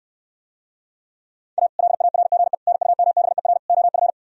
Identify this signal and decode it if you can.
Morse code, heard as a keyed tone, about 68 wpm, 700 Hz. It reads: A LARGE NUMBER OF